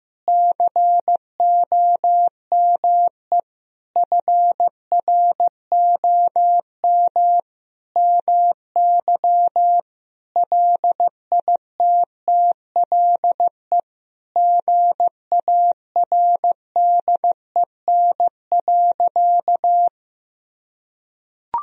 Morse code, heard as a keyed tone, about 15 wpm, 700 Hz. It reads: COME FROM MY LITTLE GARDEN.
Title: Sentences from Top 1000 Words - Review Part 4 - 15wpm